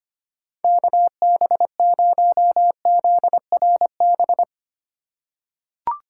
Morse code, heard as a keyed tone, about 25 wpm, 700 Hz. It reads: KB0ZRB
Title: USA Call Signs - 6 Characters - 3x Character Spacing - 25wpm